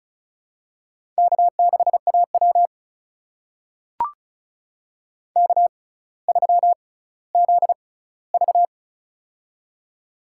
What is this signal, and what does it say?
Morse code, heard as a keyed tone, about 36 wpm, spaced as for 18 wpm, 700 Hz. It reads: K6AW K3ZV
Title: USA Call Signs - 4 Characters - 6x Character Spacing - 35wpm